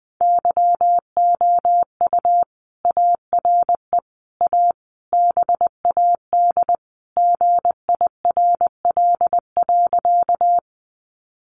Morse code, heard as a keyed tone, about 20 wpm, 700 Hz. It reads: YOU ARE A BAD GIRL.
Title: Sentences from Top 700 Words 20wpm